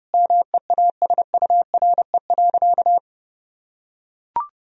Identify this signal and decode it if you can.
Morse code, heard as a keyed tone, about 30 wpm, 700 Hz. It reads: MEASURE.
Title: Sentences from Top 1000 Words - Review Part 3 - 30wpm